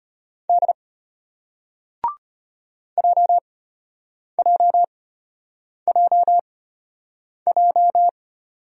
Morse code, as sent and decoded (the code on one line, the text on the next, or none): -.. / .--- .--- .--- .---
D JJJJ